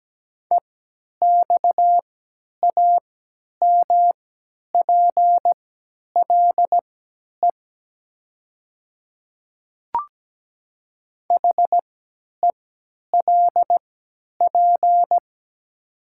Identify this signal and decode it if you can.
Morse code, heard as a keyed tone, about 17 wpm, spaced as for 10 wpm, 700 Hz. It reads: EXAMPLE HELP
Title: 3x Character Spacing - Top 300 Words - 17wpm